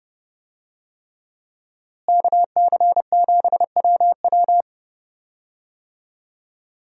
Morse code, heard as a keyed tone, about 32 wpm, 700 Hz. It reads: KC7WW
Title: USA Call Signs - 5 Characters - Repeated 3x - 30wpm